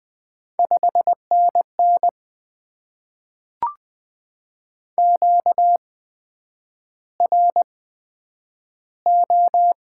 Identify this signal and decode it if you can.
Morse code, heard as a keyed tone, about 20 wpm, spaced as for 9 wpm, 700 Hz. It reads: 5NN QRO
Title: Top 90 QSO Elements - 8x Character Spacing - 20wpm